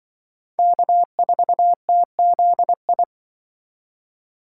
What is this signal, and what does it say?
Morse code, heard as a keyed tone, about 25 wpm, 700 Hz. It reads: K4TZI
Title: Speed Racing - USA 1x3 Call Signs 20wpm